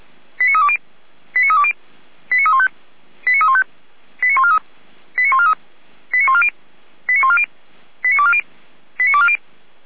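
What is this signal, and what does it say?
Five-tone selective call (ZVEI), heard as quick runs of five tones, tones of about 70 ms. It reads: 89320 89320 89315 89315 89143 89143 89140 89140 89240 89240
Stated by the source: ZVEI five-tone selective call 89305, received via SDR.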